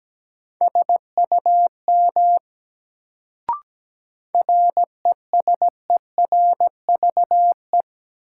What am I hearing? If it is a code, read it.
Morse code, heard as a keyed tone, about 17 wpm, 700 Hz. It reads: SUM RESERVE
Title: Sentences from Top 1000 Words - Review Part 6 - New Word Drill - 17wpm